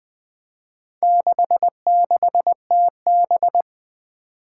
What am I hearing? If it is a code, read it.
Morse code, heard as a keyed tone, about 20 wpm, 700 Hz. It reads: 66TB
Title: Four Challenging Letter-Number Combinations 20wpm